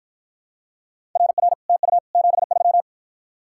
Morse code, heard as a keyed tone, about 51 wpm, 700 Hz. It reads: RR TU 73